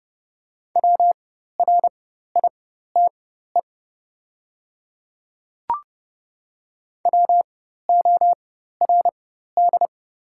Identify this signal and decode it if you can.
Morse code, heard as a keyed tone, about 30 wpm, spaced as for 14 wpm, 700 Hz. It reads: WRITE WORD
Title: 4x Character Spacing - Top 200 Words - 30wpm